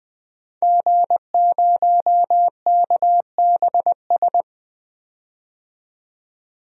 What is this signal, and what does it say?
Morse code, heard as a keyed tone, about 20 wpm, 700 Hz. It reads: G0KBS
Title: DXCC-100 Call Signs - 5 Characters - 20wpm